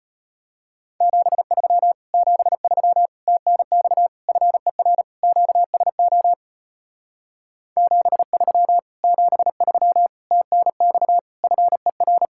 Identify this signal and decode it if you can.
Morse code, heard as a keyed tone, about 32 wpm, 700 Hz. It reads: 73 73 TNX FER QSO 73 73 TNX FER